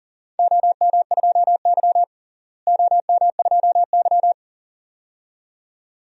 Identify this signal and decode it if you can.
Morse code, heard as a keyed tone, about 40 wpm, 700 Hz. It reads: OM2Y OM2Y
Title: DXCC-100 Call Signs - 4 Characters - Repeated 2x - 40wpm